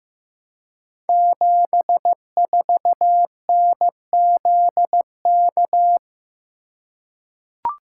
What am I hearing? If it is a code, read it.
Morse code, heard as a keyed tone, about 15 wpm, 700 Hz. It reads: 74NZK
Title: Code Groups - 5 Letter:Numbers - 15wpm